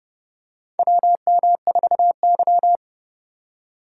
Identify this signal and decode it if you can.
Morse code, heard as a keyed tone, about 31 wpm, 700 Hz. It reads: WM4Y
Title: USA Call Signs - 3x Character Spacing - 30wpm